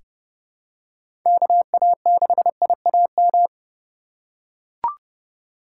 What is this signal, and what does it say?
Morse code, heard as a keyed tone, about 30 wpm, 700 Hz. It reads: KA6IAM
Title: USA Call Signs - 6 Characters - 3x Character Spacing - 30wpm